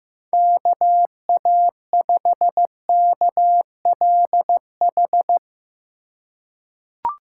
Morse code, heard as a keyed tone, about 15 wpm, 700 Hz. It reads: KA5KLH
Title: USA Call Signs - 4x Character Spacing - 15wpm